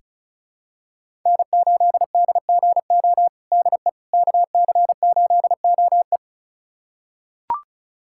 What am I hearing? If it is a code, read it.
Morse code, heard as a keyed tone, about 34 wpm, 700 Hz. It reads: N8DGO DE KC8OE
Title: USA Callsign DE Callsign - 5 Characters - 35wpm